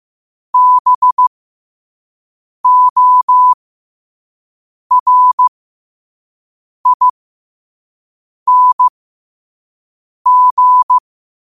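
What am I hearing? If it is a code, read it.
Morse code, heard as a keyed tone, about 15 wpm, spaced as for 5 wpm, 1000 Hz. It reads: BORING
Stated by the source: Morse quote-of-the-day practice at 5 wpm.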